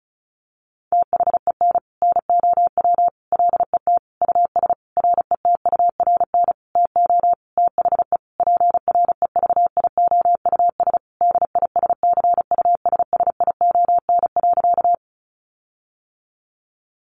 Morse code, heard as a keyed tone, about 34 wpm, 700 Hz. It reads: THEN NOW LET US RETURN TO THE PREVIOUS DISCUSSION.
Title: Sentences from Top 1500 Words 35wpm